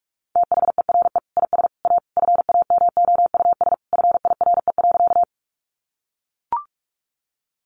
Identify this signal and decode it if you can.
Morse code, heard as a keyed tone, about 44 wpm, 700 Hz. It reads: THERE IS A FAMOUS FIRE.
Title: Sentences from Top 1500 Words 45wpm